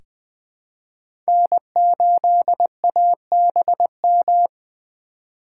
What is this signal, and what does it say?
Morse code, heard as a keyed tone, about 20 wpm, 700 Hz. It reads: N8ABM
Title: USA Call Signs - 5 Characters - 3x Character Spacing - 20wpm